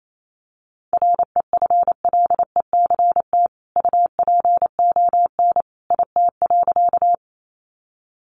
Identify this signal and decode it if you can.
Morse code, heard as a keyed tone, about 28 wpm, 700 Hz. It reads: REFLECT UPON IT.
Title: Sentences from Top 1500 Words 28wpm